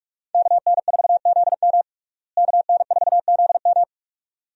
Morse code, heard as a keyed tone, about 46 wpm, 700 Hz. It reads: KN4ZM KN4ZM
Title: USA Call Signs - 5 Characters - Repeated 2x - 45wpm